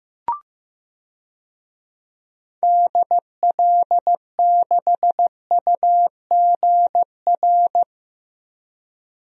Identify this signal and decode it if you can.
Morse code, heard as a keyed tone, about 15 wpm, 700 Hz. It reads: DL6UGR
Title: DXCC-100 Call Signs - 6 Characters - Repeated 3x - 15wpm